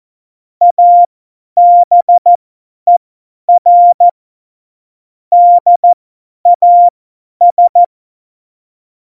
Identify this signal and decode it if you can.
Morse code, heard as a keyed tone, about 14 wpm, spaced as for 10 wpm, 700 Hz. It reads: ABER DAS
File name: Morsekurs / Witze / Witz 09. April 2019, 50 F70